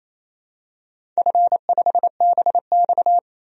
Morse code, heard as a keyed tone, about 28 wpm, 700 Hz. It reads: F5BX
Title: DXCC-100 Call Signs - 4 Characters - 28wpm